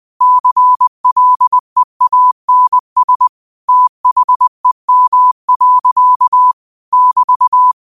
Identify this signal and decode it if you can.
Morse code, heard as a keyed tone, about 20 wpm, 1000 Hz. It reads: CLEANS THEM. <BT>
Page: New quotes every day in morse code at 20 Words per minute.